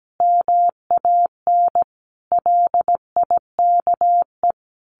Morse code, heard as a keyed tone, about 17 wpm, 700 Hz. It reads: MAN LIKE
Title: Sentences from Top 1200 Words 17wpm